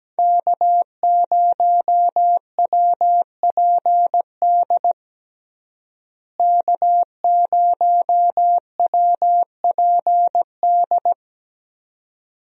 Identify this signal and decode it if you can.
Morse code, heard as a keyed tone, about 17 wpm, 700 Hz. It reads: K0WPD K0WPD
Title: USA Call Signs - 5 Characters - Repeated 2x - 17wpm